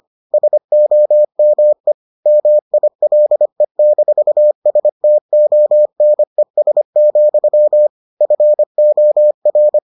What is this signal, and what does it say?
Morse code, heard as a keyed tone, about 25 wpm, 600 Hz. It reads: SOME MILE-STONES, FOR